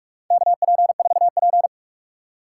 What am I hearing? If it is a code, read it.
Morse code, heard as a keyed tone, about 45 wpm, 700 Hz. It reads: KP4P
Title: USA Call Signs - 4 Characters - 3x Character Spacing - 45wpm